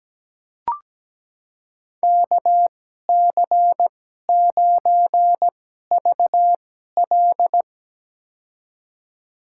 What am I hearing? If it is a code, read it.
Morse code, heard as a keyed tone, about 17 wpm, spaced as for 12 wpm, 700 Hz. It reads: KC9VL
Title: USA Call Signs - 5 Characters - 2x Character Spacing - 17wpm